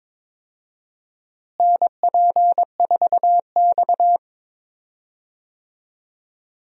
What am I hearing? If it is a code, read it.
Morse code, heard as a keyed tone, about 22 wpm, 700 Hz. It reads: NP4X
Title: USA Call Signs - 4 Characters - Repeated 3x - 22wpm